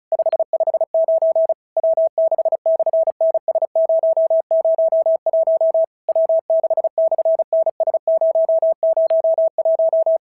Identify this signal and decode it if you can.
Morse code, heard as a keyed tone, about 35 wpm, 650 Hz. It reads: 559 W6/NS001 W6/NS001
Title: SOTA QSO Practice 35wpm